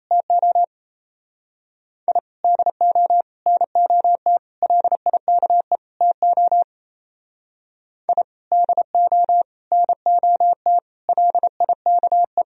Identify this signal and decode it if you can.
Morse code, heard as a keyed tone, about 35 wpm, 700 Hz. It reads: TO I DO NOT LIKE TO I DO NOT LIKE
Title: Speed-Racing - Sets of 5 Words 28wpm